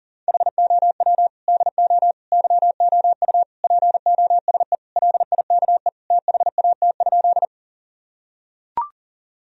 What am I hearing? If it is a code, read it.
Morse code, heard as a keyed tone, about 40 wpm, 700 Hz. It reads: HOW DO YOU POSE LIKE THAT?
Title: Sentences from Top 1000 Words - Review Part 2 - 40wpm